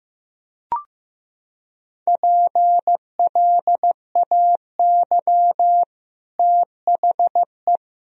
Morse code, heard as a keyed tone, about 16 wpm, 700 Hz. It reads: PLAY THE
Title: Sentences from Top 1000 Words - Review Part 3 - 15wpm